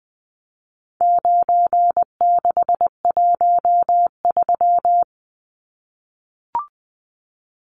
Morse code, heard as a keyed tone, about 20 wpm, 700 Hz. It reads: 9613
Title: Numbers - 4-Digit 20wpm